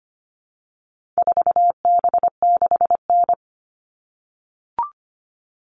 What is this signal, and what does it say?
Morse code, heard as a keyed tone, about 25 wpm, 700 Hz. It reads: 4B6N
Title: All Cut Numbers - 4-Digit 25wpm